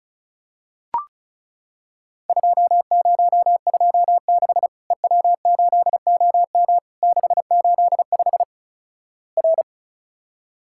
Morse code, heard as a keyed tone, about 34 wpm, 700 Hz. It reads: 2026 EW8OM 685 R
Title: Worked All Europe DX Contest - QTC Exchange - 35wpm